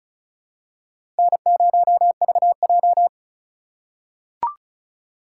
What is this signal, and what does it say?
Morse code, heard as a keyed tone, about 36 wpm, 700 Hz. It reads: N0VJ
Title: USA Call Signs - 3x Character Spacing - 35wpm